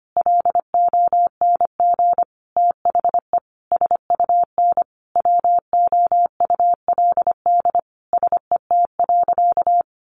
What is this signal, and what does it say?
Morse code, heard as a keyed tone, about 25 wpm, 700 Hz. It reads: LONG THE SUN WOULD SET.
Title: Sentences from Top 400 Words 25wpm